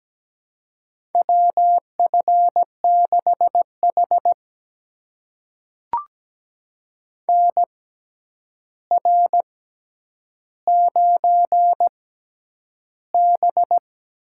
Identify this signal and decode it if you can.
Morse code, heard as a keyed tone, about 17 wpm, spaced as for 9 wpm, 700 Hz. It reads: WF6H NR9B
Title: USA Call Signs - 4 Characters - 6x Character Spacing - 17wpm